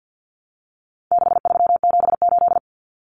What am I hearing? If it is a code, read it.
Morse code, heard as a keyed tone, about 48 wpm, 700 Hz. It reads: 6378
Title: Numbers - 4-Digit 50wpm